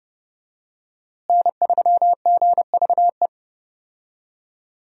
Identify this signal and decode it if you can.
Morse code, heard as a keyed tone, about 30 wpm, 700 Hz. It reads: N3GVE
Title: USA Call Signs - 5 Characters - Repeated 3x - 30wpm